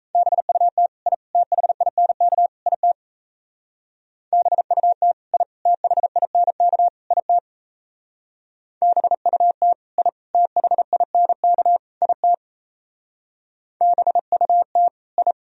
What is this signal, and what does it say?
Morse code, heard as a keyed tone, about 33 wpm, 700 Hz. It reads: BUT I THINK IT BUT I THINK IT BUT I THINK IT BUT I
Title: Speed-Racing - Sets of 4 Words 28wpm